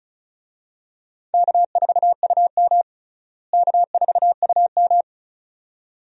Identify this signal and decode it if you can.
Morse code, heard as a keyed tone, about 34 wpm, 700 Hz. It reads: K4UM K4UM
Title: USA Call Signs - 4 Characters - Repeated 2x - 35wpm